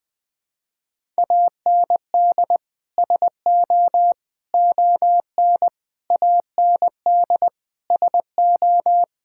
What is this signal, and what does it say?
Morse code, heard as a keyed tone, about 20 wpm, 700 Hz. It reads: AND SO ON AND SO
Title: Sets of 6 Words - 20wpm